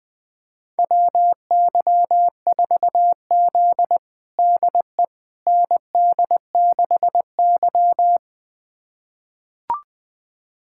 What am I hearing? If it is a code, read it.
Morse code, heard as a keyed tone, about 20 wpm, 700 Hz. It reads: WY4Z DE ND6Y
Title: USA Callsign DE Callsign - 4 Characters - 20wpm